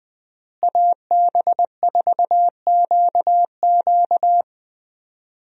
Morse code, heard as a keyed tone, about 20 wpm, 700 Hz. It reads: AB4QQ
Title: USA Call Signs - 5 Characters - Repeated 2x - 20wpm